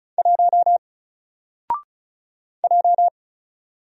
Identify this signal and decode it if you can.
Morse code, heard as a keyed tone, about 34 wpm, 700 Hz. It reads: 1 J
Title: Single Character (Letter:Number:Punctuation:Prosign) Alt 35wpm